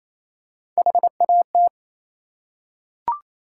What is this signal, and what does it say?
Morse code, heard as a keyed tone, about 28 wpm, 700 Hz. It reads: HAT